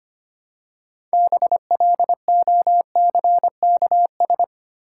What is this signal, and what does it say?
Morse code, heard as a keyed tone, about 25 wpm, 690 Hz. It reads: BLOCKS
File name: practice / 25wpm / Easy Words with Hints - 25wpm